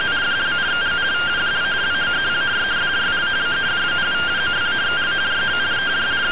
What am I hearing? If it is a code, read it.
Two-tone teleprinter signal, about 170 Hz shift.